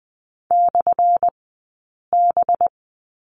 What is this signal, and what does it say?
Morse code, heard as a keyed tone, about 20 wpm, spaced as for 8 wpm, 700 Hz. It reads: /B
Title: 197 Lesson 32d - Letter k - Two Characters 20wpm